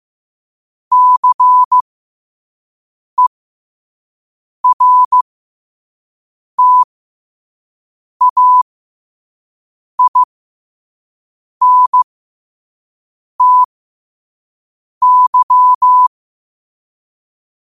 Morse code, heard as a keyed tone, about 15 wpm, spaced as for 5 wpm, 1000 Hz. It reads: CERTAINTY
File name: qotdfiles / 5wpm